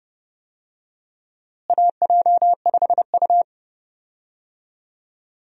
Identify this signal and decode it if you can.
Morse code, heard as a keyed tone, about 30 wpm, 700 Hz. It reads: AJ5U
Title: USA Call Signs - 4 Characters - Repeated 3x - 30wpm